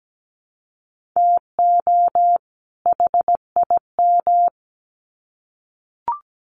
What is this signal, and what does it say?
Morse code, heard as a keyed tone, about 17 wpm, 700 Hz. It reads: TO HIM
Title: Sets of 2 Words 17wpm